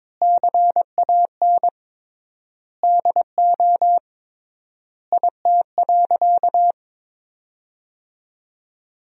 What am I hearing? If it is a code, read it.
Morse code, heard as a keyed tone, about 22 wpm, 700 Hz. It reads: CAN DO IT.
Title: 3x Word-Spacing - Sentences from Top 100 Words - 22wpm